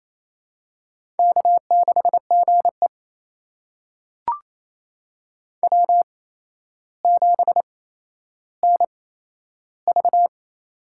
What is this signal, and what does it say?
Morse code, heard as a keyed tone, about 27 wpm, spaced as for 12 wpm, 700 Hz. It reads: K6GE W7NV
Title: USA Call Signs - 4 Characters - 8x Character Spacing - 28wpm